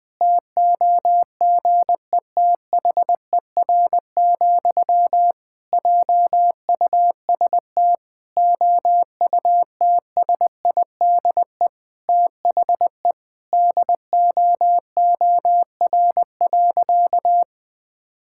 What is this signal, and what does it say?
Morse code, heard as a keyed tone, about 20 wpm, 700 Hz. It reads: TOGETHER, JUST OUTSIDE THE DOOR.
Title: Sentences from Top 1000 Words - Review Part 6 - 20wpm